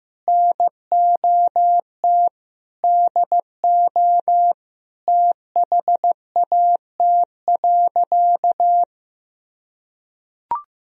Morse code, heard as a keyed tone, about 15 wpm, 700 Hz. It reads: NOT DO THAT.
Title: Sentences from Top 1000 Words - Review Part 4 - 15wpm